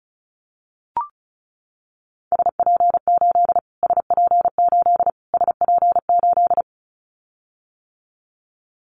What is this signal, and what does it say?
Morse code, heard as a keyed tone, about 34 wpm, 700 Hz. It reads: SP8 SP8 SP8
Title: Callsign Prefixes for Top 25 DXCC - Repeated 3x - 35wpm